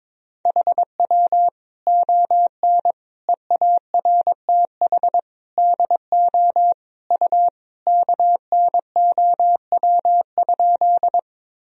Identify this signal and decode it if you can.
Morse code, heard as a keyed tone, about 22 wpm, 700 Hz. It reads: HW ON EARTH DO U KNOW?
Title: Sentences from Top 200 Words and Abbreviations 22wpm